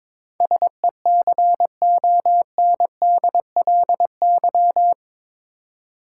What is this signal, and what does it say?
Morse code, heard as a keyed tone, about 21 wpm, 700 Hz. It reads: SECONDLY
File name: practice / 22wpm / Eight Letter Words - 22wpm